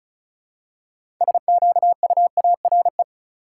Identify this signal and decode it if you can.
Morse code, heard as a keyed tone, about 34 wpm, 700 Hz. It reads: SQUARE
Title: Six Letter Words - 35wpm